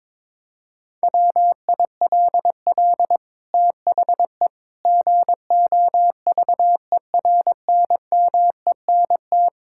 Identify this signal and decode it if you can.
Morse code, heard as a keyed tone, about 22 wpm, 700 Hz. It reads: WILL THE GOVERNMENT
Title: Sentences from Top 1000 Words - Review Part 6 - 22wpm